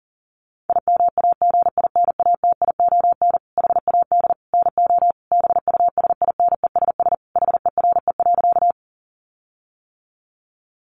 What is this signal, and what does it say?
Morse code, heard as a keyed tone, about 40 wpm, 700 Hz. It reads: IMAGINATION HAD NO BUSINESS HERE.
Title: Sentences from Top 1300 Words 40wpm